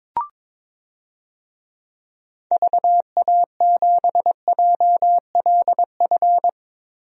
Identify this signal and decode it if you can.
Morse code, heard as a keyed tone, about 22 wpm, 700 Hz. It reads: VA7JLF
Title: DXCC-100 Call Signs - 6 Characters - Repeated 2x - 22wpm